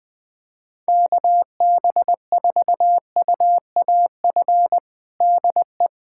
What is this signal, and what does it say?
Morse code, heard as a keyed tone, about 20 wpm, 700 Hz. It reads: KB4UAF DE
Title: USA Callsign DE Callsign 20wpm